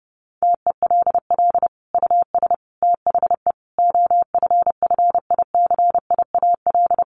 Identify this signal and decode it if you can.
Morse code, heard as a keyed tone, about 30 wpm, 700 Hz. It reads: TELL US THE OFFICIAL
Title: Sentences from Top 1400 Words 30wpm